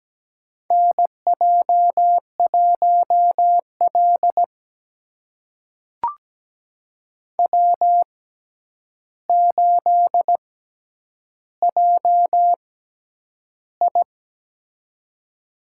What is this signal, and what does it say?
Morse code, heard as a keyed tone, about 17 wpm, spaced as for 9 wpm, 700 Hz. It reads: NJ1L W8JI
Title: USA Call Signs - 4 Characters - 6x Character Spacing - 17wpm